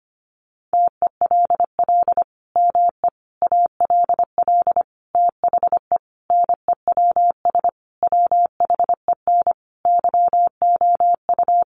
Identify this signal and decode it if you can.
Morse code, heard as a keyed tone, about 25 wpm, 700 Hz. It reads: TELL ME ALL THE NEWS WHEN YOU
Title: Sentences from Top 800 Words 25wpm